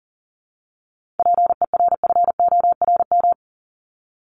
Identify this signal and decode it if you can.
Morse code, heard as a keyed tone, about 40 wpm, 700 Hz. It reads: PERFORM
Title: Top 751-1000 Words 40wpm